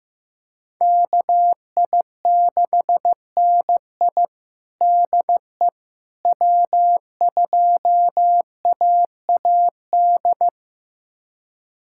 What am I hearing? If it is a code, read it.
Morse code, heard as a keyed tone, about 15 wpm, 700 Hz. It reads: KI6NI DE W2AAD
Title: USA Callsign DE Callsign - 5 Characters - 15wpm